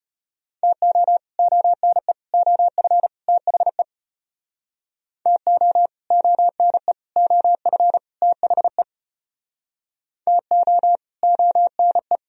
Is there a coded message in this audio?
Yes: Morse code from 0.6 to 12.2 s, heard as a keyed tone, about 32 wpm, 700 Hz. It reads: TO ONE OF THE TO ONE OF THE TO ONE